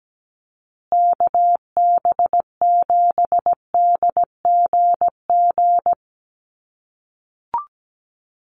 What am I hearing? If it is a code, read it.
Morse code, heard as a keyed tone, about 17 wpm, 700 Hz. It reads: KB7DGG
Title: USA Call Signs Repeated 2x 17wpm